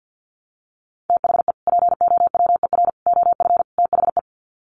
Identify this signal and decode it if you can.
Morse code, heard as a keyed tone, about 50 wpm, 700 Hz. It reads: THE POWER OF THE
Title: Sets of 4 Words 50wpm